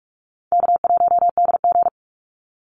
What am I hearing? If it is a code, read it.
Morse code, heard as a keyed tone, about 46 wpm, 700 Hz. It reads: K1DG
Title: Easy Call Signs - 45wpm